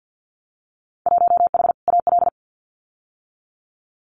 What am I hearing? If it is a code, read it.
Morse code, heard as a keyed tone, about 50 wpm, 700 Hz. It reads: 1H AL
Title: Winter Field Day Exchanges 50wpm